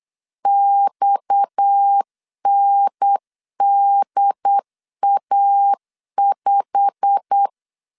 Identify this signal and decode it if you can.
Morse code, heard as a keyed tone, about 8 wpm, 800 Hz. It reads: XNDA5
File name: LEZIONI / 10 SIGNS / 10C DOT 8wpm 3649